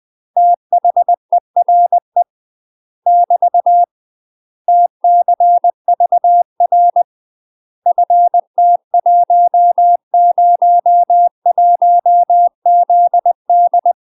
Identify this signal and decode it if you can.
Morse code, heard as a keyed tone, about 20 wpm, 700 Hz. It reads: THERE <BT> TCVR FT101ZD
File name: 20WPM